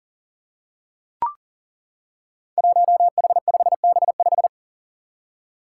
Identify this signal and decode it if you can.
Morse code, heard as a keyed tone, about 38 wpm, 700 Hz. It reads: 1H5B5
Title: Five Challenging Letter-Number Combinations 40wpm